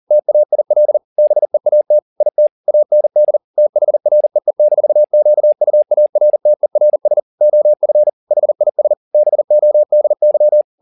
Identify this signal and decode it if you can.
Morse code, heard as a keyed tone, about 40 wpm, 600 Hz. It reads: TAIL BEAT IT AND THREE-QUARTERS OF HIS BODY
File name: the old man and the sea 2 40wpm